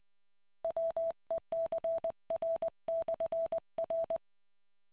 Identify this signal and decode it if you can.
Morse code, heard as a keyed tone, about 20 wpm, 660 Hz. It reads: WECR/R